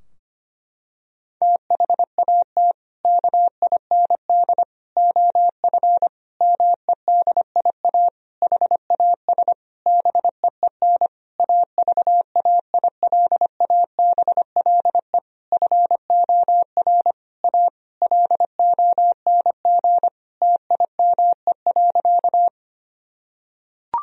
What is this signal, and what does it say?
Morse code, heard as a keyed tone, about 25 wpm, 700 Hz. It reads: THAT KIND OF MEDIA HAS BEEN AVAILABLE FOR A LONG TIME.
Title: Sentences from Top 1000 Words - Review Part 4 - 25wpm